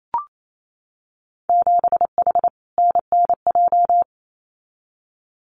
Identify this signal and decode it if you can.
Morse code, heard as a keyed tone, about 28 wpm, 700 Hz. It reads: 7H NNJ
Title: Winter Field Day Exchanges Encore 28wpm